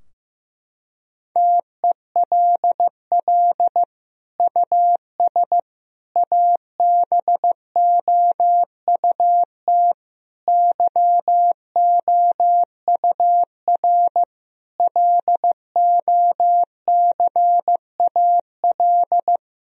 Morse code, heard as a keyed tone, about 16 wpm, 700 Hz. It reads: TELL US ABOUT YOUR LOCAL
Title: Sentences from Top 1000 Words - Review Part 3 - 15wpm